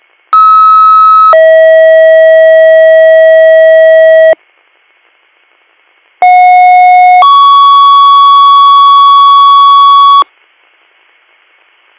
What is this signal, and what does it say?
Two-tone page, heard as a one-second tone, then a three-second tone, about 1290 then 640 Hz, about 730 then 1120 Hz. It is heard from 0.3 to 10.2 s.